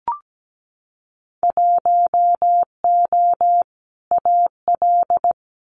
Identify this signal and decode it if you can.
Morse code, heard as a keyed tone, about 17 wpm, 700 Hz. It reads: 1O AL
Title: Winter Field Day Exchanges 17wpm